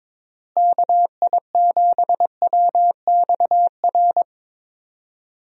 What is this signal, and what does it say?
Morse code, heard as a keyed tone, about 21 wpm, 700 Hz. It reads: KI7WXR
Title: USA Call Signs - 6 Characters - Repeated 3x - 22wpm